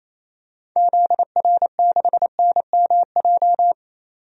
Morse code, heard as a keyed tone, about 28 wpm, 700 Hz. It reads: ZR6NMJ
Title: DXCC-100 Call Signs - 6 Characters - 28wpm